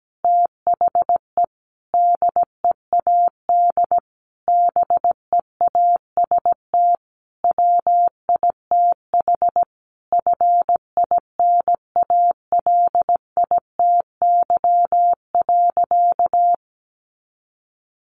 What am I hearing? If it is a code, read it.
Morse code, heard as a keyed tone, about 17 wpm, 700 Hz. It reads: THE DEAD BEAST WITH FINALITY.
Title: A Sound Of Thunder 17wpm